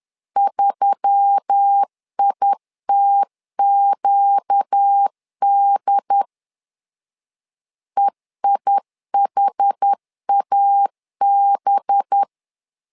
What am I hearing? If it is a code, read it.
Morse code, heard as a keyed tone, about 10 wpm, 800 Hz. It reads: 3ITQD EIHAB
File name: LEZIONI / 5 LBVC / 5 LBV 10wpm 2248